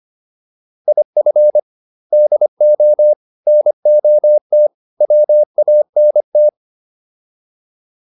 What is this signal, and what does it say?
Morse code, heard as a keyed tone, about 25 wpm, 600 Hz. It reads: IF DO NOT WANT